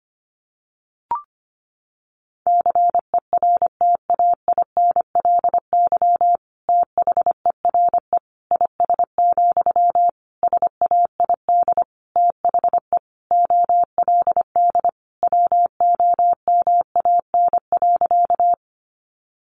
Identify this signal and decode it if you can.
Morse code, heard as a keyed tone, about 25 wpm, 700 Hz. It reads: CERTAINLY THERE IS, SAID THE OLD WOMAN.
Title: Sentences from Top 1100 Words 25wpm